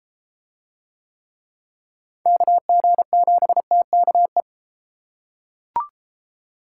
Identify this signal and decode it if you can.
Morse code, heard as a keyed tone, about 29 wpm, 700 Hz. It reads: KG7TKE